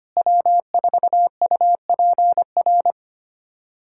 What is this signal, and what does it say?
Morse code, heard as a keyed tone, about 25 wpm, 700 Hz. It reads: W4UPR
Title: USA Callsign DE Callsign 25wpm